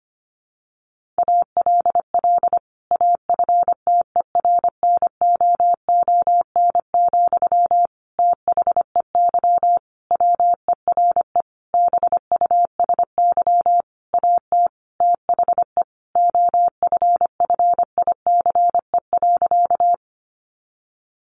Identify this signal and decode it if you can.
Morse code, heard as a keyed tone, about 25 wpm, 700 Hz. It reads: ALL AFTERNOON, THEY WERE BUSY AT THE OFFICE.
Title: Sentences from Top 1500 Words 25wpm